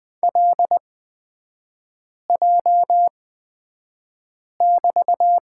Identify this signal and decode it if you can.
Morse code, heard as a keyed tone, about 20 wpm, spaced as for 5 wpm, 700 Hz. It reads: LJ<BT>
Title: 224 Lesson 36e - BT - Three Characters 20wpm